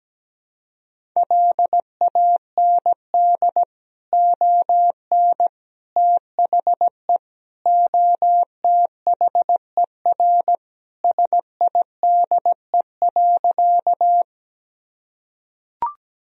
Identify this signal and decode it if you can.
Morse code, heard as a keyed tone, about 17 wpm, 700 Hz. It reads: LAND ON THE OTHER SIDE.
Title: Sentences from Top 1000 Words - Review Part 1 - 17wpm